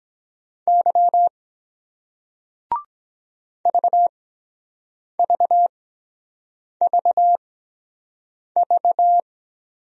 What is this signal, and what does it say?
Morse code, heard as a keyed tone, about 23 wpm, spaced as for 7 wpm, 700 Hz. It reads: Y VVVV